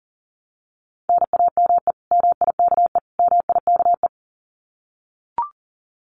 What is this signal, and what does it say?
Morse code, heard as a keyed tone, about 40 wpm, 700 Hz. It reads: NAME MIKE MIKE
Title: Common QSO Phrases Encore 40wpm